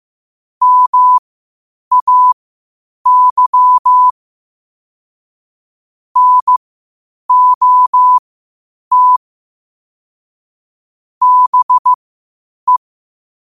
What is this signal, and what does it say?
Morse code, heard as a keyed tone, about 15 wpm, spaced as for 8 wpm, 1000 Hz. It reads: MAY NOT BE
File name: qotdfiles / 8wpm